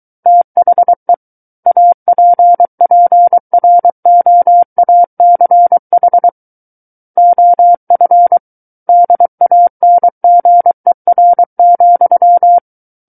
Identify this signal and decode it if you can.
Morse code, heard as a keyed tone, about 23 wpm, 700 Hz. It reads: THE APPROACH OF DANGER,
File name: Book One / 14-Chapter 14 - 23 WPM